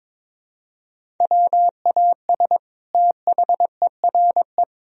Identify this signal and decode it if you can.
Morse code, heard as a keyed tone, about 22 wpm, 700 Hz. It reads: WAS THERE
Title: Sentences from Top 1000 Words - Review Part 3 - 22wpm